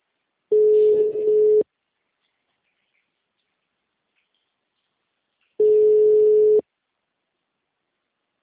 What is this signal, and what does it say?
Ringback tone (Europe, 425 Hz).